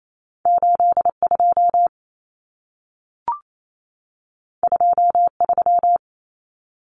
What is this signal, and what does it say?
Morse code, heard as a keyed tone, about 28 wpm, 700 Hz. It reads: 82 23
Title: Numbers - 2-Digit 28wpm